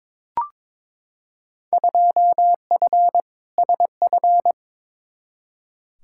Morse code, heard as a keyed tone, about 22 wpm, 700 Hz. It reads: 2F SF
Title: Field Day Exchanges - 22wpm